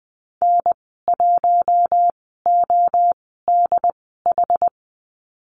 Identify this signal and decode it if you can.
Morse code, heard as a keyed tone, about 20 wpm, spaced as for 14 wpm, 700 Hz. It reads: N1ODH
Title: 161 Lesson 26g - Letter g - Call Signs 20wpm